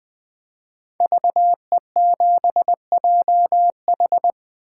Morse code, heard as a keyed tone, about 20 wpm, 700 Hz. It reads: VE7JH
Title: 2023 SOTA Chaser Call Signs - 20wpm